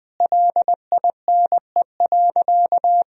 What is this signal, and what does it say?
Morse code, heard as a keyed tone, about 20 wpm, 700 Hz. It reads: LINE.
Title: Sentences from Top 1000 Words - Review Part 1 - 20wpm